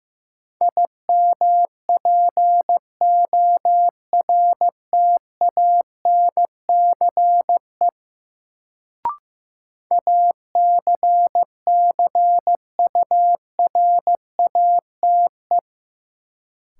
Morse code, heard as a keyed tone, about 15 wpm, 700 Hz. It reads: IMPORTANCE ACCURATE
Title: Sentences from Top 1200 Words - New Word Drill Alt - 15wpm